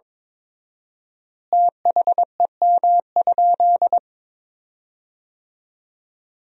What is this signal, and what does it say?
Morse code, heard as a keyed tone, about 22 wpm, 700 Hz. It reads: THEM?
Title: 4x Word-Spacing - Sentences from Top 100 Words - 22wpm